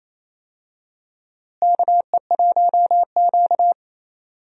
Morse code, heard as a keyed tone, about 28 wpm, 700 Hz. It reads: KE1Q
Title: USA Call Signs - 4 Characters - Repeated 2x - 28wpm